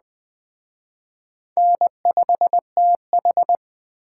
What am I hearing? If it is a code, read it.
Morse code, heard as a keyed tone, about 20 wpm, 700 Hz. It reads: N5TH
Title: Speed Racing - USA Call Signs 17wpm